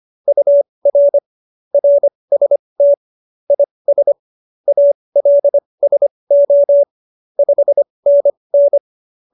Morse code, heard as a keyed tone, about 25 wpm, spaced as for 20 wpm, 570 Hz. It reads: UR RST IS ALSO 5NN
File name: QSO301 20